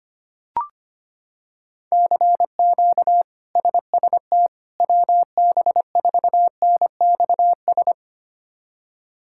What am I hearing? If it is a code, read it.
Morse code, heard as a keyed tone, about 25 wpm, 700 Hz. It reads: CQ SST WB4NXS
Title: K1USN SST QSO Practice - 25wpm